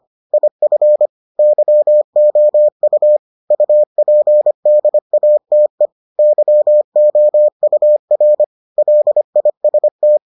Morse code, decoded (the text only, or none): IF YOU UPDATE YOUR LIST